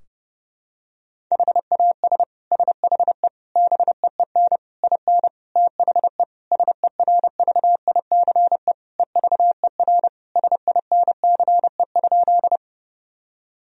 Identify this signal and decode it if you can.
Morse code, heard as a keyed tone, about 30 wpm, 700 Hz. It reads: HAS SHE BEEN IN THE SERVICE EVER SINCE?
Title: Sentences from Top 1000 Words - Review Part 4 - 30wpm